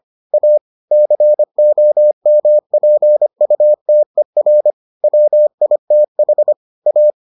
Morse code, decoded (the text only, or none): A COMPUTER WITH A